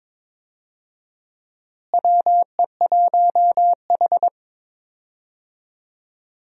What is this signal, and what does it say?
Morse code, heard as a keyed tone, about 22 wpm, 700 Hz. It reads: WE1H
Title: USA Call Signs - 4 Characters - Repeated 3x - 22wpm